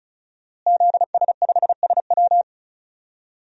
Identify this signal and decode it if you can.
Morse code, heard as a keyed tone, about 34 wpm, 700 Hz. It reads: ZS5SW
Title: DXCC-100 Call Signs - 5 Characters - Repeated 2x - 35wpm